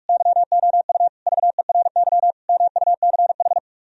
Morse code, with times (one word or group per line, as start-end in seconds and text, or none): 0.1-1.1 YOU
1.3-2.3 VERY
2.5-3.6 MUCH